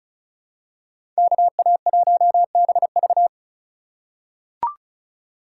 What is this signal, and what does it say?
Morse code, heard as a keyed tone, about 34 wpm, 700 Hz. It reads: KA1BV
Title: USA Call Signs - 5 Characters - 4x Character Spacing - 35wpm